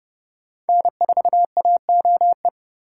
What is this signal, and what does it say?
Morse code, heard as a keyed tone, about 31 wpm, 700 Hz. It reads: N4AOE